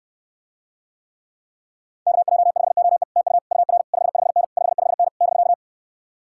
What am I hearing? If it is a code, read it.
Morse code, heard as a keyed tone, about 69 wpm, 700 Hz. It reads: KJ5OE TU UR 55N 55N <BK>